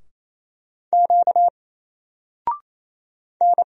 Morse code, heard as a keyed tone, about 27 wpm, 700 Hz. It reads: Q N
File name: Single-Letter-Number-28wpm